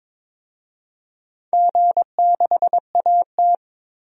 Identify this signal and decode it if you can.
Morse code, heard as a keyed tone, about 22 wpm, 700 Hz. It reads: G6AT
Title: DXCC-100 Call Signs - 4 Characters - 22wpm